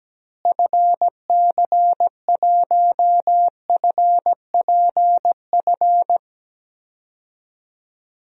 Morse code, heard as a keyed tone, about 17 wpm, 700 Hz. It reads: FC1FPF
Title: DXCC-100 Call Signs - 6 Characters - Repeated 3x - 17wpm